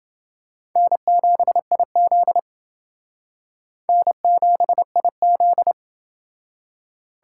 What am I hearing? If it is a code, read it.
Morse code, heard as a keyed tone, about 31 wpm, 700 Hz. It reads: N7IZ N7IZ